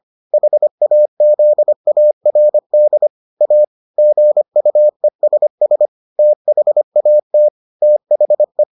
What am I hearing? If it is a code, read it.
Morse code, heard as a keyed tone, about 25 wpm, 600 Hz. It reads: HAZARD A GUESS THAT THE